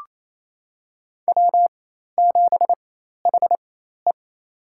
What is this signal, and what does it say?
Morse code, heard as a keyed tone, about 28 wpm, spaced as for 13 wpm, 700 Hz. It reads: W7HE
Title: USA Call Signs - 4 Characters - 4x Character Spacing - 28wpm